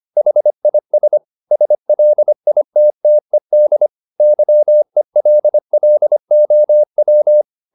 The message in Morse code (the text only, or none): HIS SLITTED YELLOW